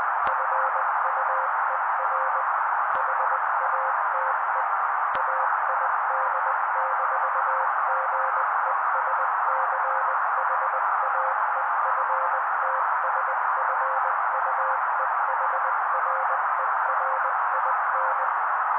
Morse code, heard as a keyed tone, about 20 wpm, 530 Hz. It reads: FUER HATE AID-GESCHAEFTSFUEHRERIN
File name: daily / 18wpm noise